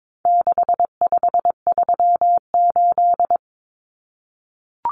Morse code, heard as a keyed tone, about 22 wpm, 700 Hz. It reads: 6538
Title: Common Cut Numbers - 4-Digit 22wpm